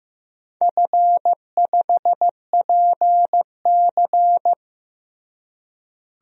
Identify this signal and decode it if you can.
Morse code, heard as a keyed tone, about 16 wpm, 700 Hz. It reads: F5PC
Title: DXCC-100 Call Signs - 4 Characters - Repeated 3x - 15wpm